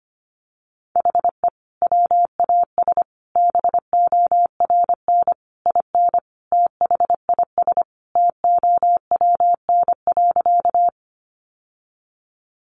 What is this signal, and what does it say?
Morse code, heard as a keyed tone, about 25 wpm, 700 Hz. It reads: HE WAS BORN IN THIS TOWN.
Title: Sentences from Top 1000 Words 25wpm